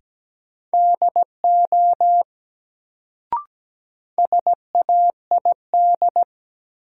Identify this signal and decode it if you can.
Morse code, heard as a keyed tone, about 17 wpm, 700 Hz. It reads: DO SAID
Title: Sentences from Top 100 Words - New Word Drill - 17wpm